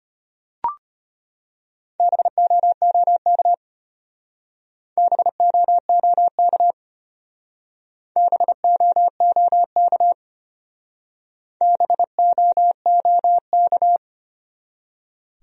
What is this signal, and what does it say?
Morse code, heard as a keyed tone, about 30 wpm, 700 Hz. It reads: BOOK BOOK BOOK BOOK
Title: Speed-Racing Top 300 Words 25wpm